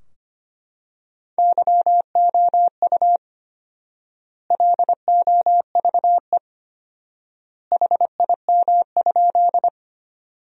Morse code, heard as a keyed tone, about 25 wpm, 700 Hz. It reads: YOU LOVE HIM?
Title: 4x Word-Spacing - Sentences from Top 400 Words - 25wpm